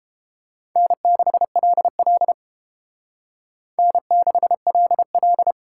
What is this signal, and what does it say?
Morse code, heard as a keyed tone, about 30 wpm, 700 Hz. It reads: N6LL N6LL